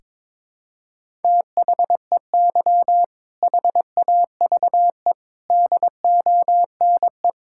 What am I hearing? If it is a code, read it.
Morse code, heard as a keyed tone, about 22 wpm, 700 Hz. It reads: THEY HAVE DONE
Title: Sentences from Top 1000 Words - Review Part 5 - 22wpm